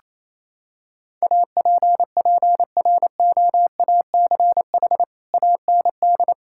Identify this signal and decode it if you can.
Morse code, heard as a keyed tone, about 28 wpm, 700 Hz. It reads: APPROACH AND
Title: Sentences from Top 1600 Words 28wpm